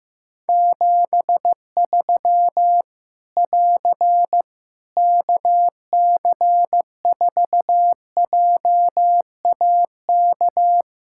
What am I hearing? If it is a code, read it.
Morse code, heard as a keyed tone, about 15 wpm, 700 Hz. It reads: 73 <AR> KC4JAK
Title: Practice QSO - Exchange 4 - Ending - 15wpm